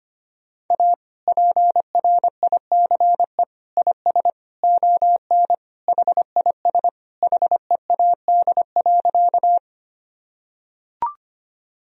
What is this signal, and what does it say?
Morse code, heard as a keyed tone, about 25 wpm, 700 Hz. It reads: A PRICE IS ON HIS HEAD.
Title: Sentences from Top 1000 Words - Review Part 4 - 25wpm